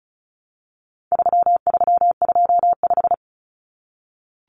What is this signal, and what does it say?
Morse code, heard as a keyed tone, about 34 wpm, 700 Hz. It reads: 3325
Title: Common Cut Numbers - 4-Digit 35wpm